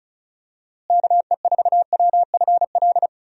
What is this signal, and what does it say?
Morse code, heard as a keyed tone, about 34 wpm, 700 Hz. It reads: KE4WFL
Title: USA Call Signs - 6 Characters - Repeated 3x - 35wpm